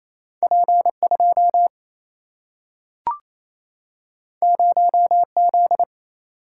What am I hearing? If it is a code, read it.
Morse code, heard as a keyed tone, about 28 wpm, 700 Hz. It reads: P 2 0 Z
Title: Single-Letter Single-Number 28wpm